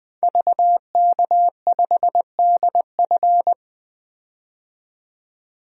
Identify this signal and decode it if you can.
Morse code, heard as a keyed tone, about 21 wpm, 700 Hz. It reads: VK5DF
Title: DXCC-100 Call Signs - 5 Characters - 20wpm